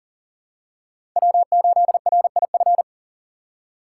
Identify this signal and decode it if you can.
Morse code, heard as a keyed tone, about 40 wpm, 700 Hz. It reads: W8RIF
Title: USA Call Signs - 5 Characters - 4x Character Spacing - 40wpm